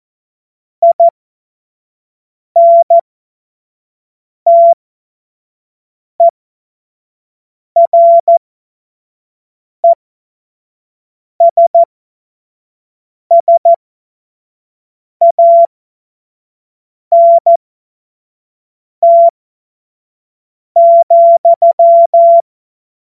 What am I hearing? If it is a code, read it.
Morse code, heard as a keyed tone, about 14 wpm, spaced as for 5 wpm, 680 Hz. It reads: INTERESSANT,
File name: Witz-29-Juli-2025-25-F70